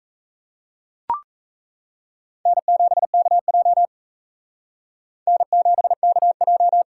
Morse code, heard as a keyed tone, about 36 wpm, 700 Hz. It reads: N7KJ N7KJ